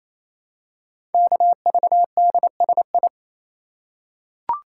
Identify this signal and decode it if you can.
Morse code, heard as a keyed tone, about 27 wpm, 700 Hz. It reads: KVDSI
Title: Code Groups - 5 Letters - 28wpm